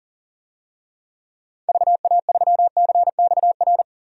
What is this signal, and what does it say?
Morse code, heard as a keyed tone, about 40 wpm, 700 Hz. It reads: VA3CXR
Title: DXCC-100 Call Signs - 6 Characters - 40wpm